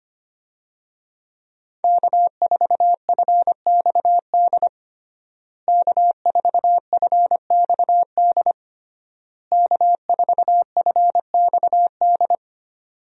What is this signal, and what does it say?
Morse code, heard as a keyed tone, about 25 wpm, 700 Hz. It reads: K4FXD K4FXD K4FXD
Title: USA Call Signs - 5 Characters - Repeated 3x - 25wpm